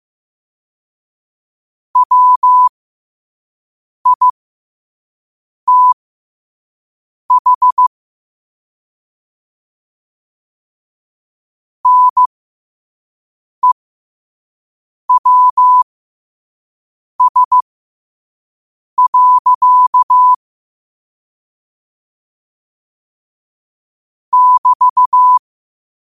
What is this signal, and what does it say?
Morse code, heard as a keyed tone, about 15 wpm, spaced as for 5 wpm, 1000 Hz. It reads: WITH NEWS. <BT>
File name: qotdfiles / 5wpm